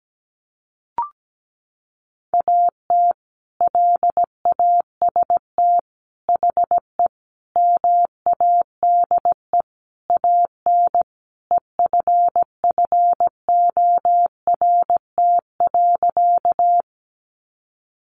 Morse code, heard as a keyed tone, about 17 wpm, 700 Hz. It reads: AT LAST HE MADE AN EFFORT.
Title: Sentences from Top 1200 Words 17wpm